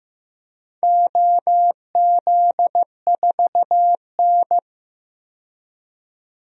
Morse code, heard as a keyed tone, about 15 wpm, 700 Hz. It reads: OZ4N
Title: DXCC-100 Call Signs - 4 Characters - 15wpm